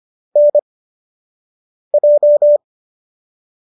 Morse code, heard as a keyed tone, about 25 wpm, spaced as for 6 wpm, 590 Hz. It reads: NJ